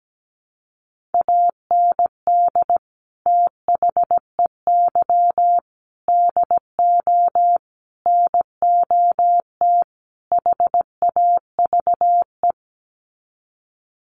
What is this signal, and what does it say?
Morse code, heard as a keyed tone, about 17 wpm, 700 Hz. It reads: AND THEY DO NOT HAVE
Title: Sets of 5 Words Encore2 17wpm